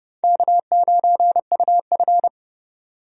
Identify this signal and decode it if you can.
Morse code, heard as a keyed tone, about 30 wpm, 700 Hz. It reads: K9UF
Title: USA Call Signs - 4 Characters - 30wpm